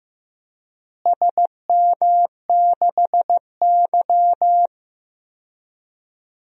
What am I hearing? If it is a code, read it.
Morse code, heard as a keyed tone, about 15 wpm, 700 Hz. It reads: SM6Y
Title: DXCC-100 Call Signs - 4 Characters - Repeated 3x - 15wpm